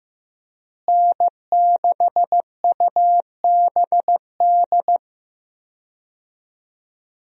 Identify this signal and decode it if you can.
Morse code, heard as a keyed tone, about 15 wpm, 700 Hz. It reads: N6UBD
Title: USA Call Signs - 5 Characters - Repeated 3x - 15wpm